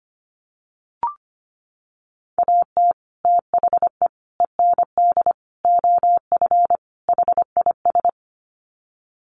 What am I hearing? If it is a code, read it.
Morse code, heard as a keyed tone, about 25 wpm, 700 Hz. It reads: AT THE END OF HIS
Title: Sets of 5 Words Encore 25wpm